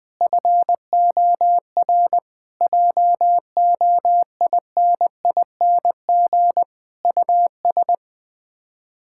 Morse code, heard as a keyed tone, about 20 wpm, 690 Hz. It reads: FOR JOINING US
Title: Build A Phrase - 5-Word Phrases - 20wpm